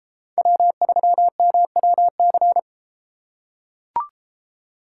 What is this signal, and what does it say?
Morse code, heard as a keyed tone, about 33 wpm, 700 Hz. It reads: W3MWC